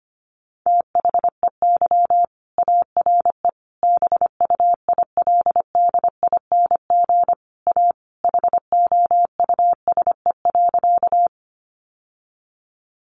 Morse code, heard as a keyed tone, about 25 wpm, 700 Hz. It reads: THEY ARE BUILDING A HOUSE.
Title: Sentences from Top 700 Words 25wpm